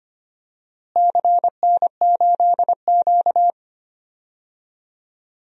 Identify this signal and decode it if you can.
Morse code, heard as a keyed tone, about 25 wpm, 700 Hz. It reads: CN8Q
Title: DXCC-100 Call Signs - 4 Characters - Repeated 2x - 25wpm